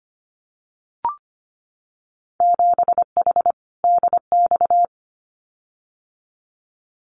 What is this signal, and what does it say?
Morse code, heard as a keyed tone, about 25 wpm, 690 Hz. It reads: 7H DX
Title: Winter Field Day Exchanges Encore 25wpm